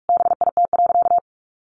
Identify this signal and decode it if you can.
Morse code, heard as a keyed tone, about 44 wpm, 700 Hz. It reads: BIT.
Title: Sentences from Top 900 Words 45wpm